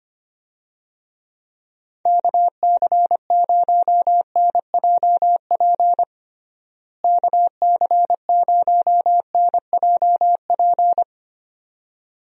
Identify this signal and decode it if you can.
Morse code, heard as a keyed tone, about 25 wpm, 700 Hz. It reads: KC0NJP KC0NJP
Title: USA Call Signs - 6 Characters - Repeated 2x - 25wpm